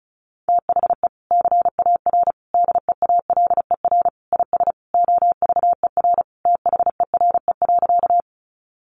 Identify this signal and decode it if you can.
Morse code, heard as a keyed tone, about 34 wpm, 700 Hz. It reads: THE CAR DEALER IS OVER THERE.
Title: Sentences from Top 1100 Words 35wpm